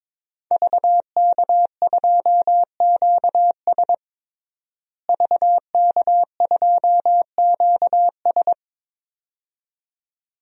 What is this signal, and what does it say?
Morse code, heard as a keyed tone, about 22 wpm, 700 Hz. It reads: VK2QS VK2QS
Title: DXCC-100 Call Signs - 5 Characters - Repeated 2x - 22wpm